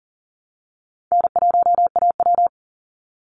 Morse code, heard as a keyed tone, about 40 wpm, 700 Hz. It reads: N1AW
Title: NA SOTA Chaser Callsigns 40wpm